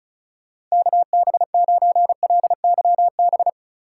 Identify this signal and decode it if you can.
Morse code, heard as a keyed tone, about 36 wpm, 700 Hz. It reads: KB9LYB
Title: USA Call Signs - 6 Characters - 3x Character Spacing - 35wpm